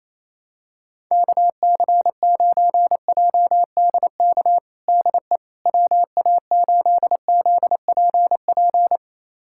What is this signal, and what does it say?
Morse code, heard as a keyed tone, about 27 wpm, 700 Hz. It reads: KC9JDK DE WA8ZPP
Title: USA Callsign DE Callsign - 6 Characters - 28wpm